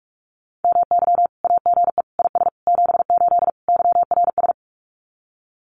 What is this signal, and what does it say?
Morse code, heard as a keyed tone, about 46 wpm, 700 Hz. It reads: MY AGE IS 78 YRS
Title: Common QSO Phrases Encore 45wpm